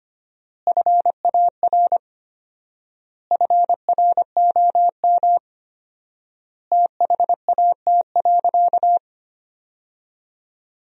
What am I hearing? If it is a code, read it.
Morse code, heard as a keyed tone, about 25 wpm, 700 Hz. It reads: FAR FROM THAT.
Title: 4x Word-Spacing - Sentences from Top 300 Words - 25wpm